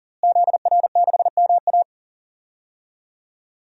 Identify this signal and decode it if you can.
Morse code, heard as a keyed tone, about 40 wpm, 700 Hz. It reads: ZR6MA
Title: DXCC-100 Call Signs - 5 Characters - Repeated 3x - 40wpm